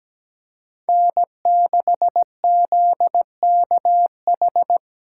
Morse code, heard as a keyed tone, about 17 wpm, 700 Hz. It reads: N6ZKH
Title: USA Call Signs - 5 Characters - 3x Character Spacing - 17wpm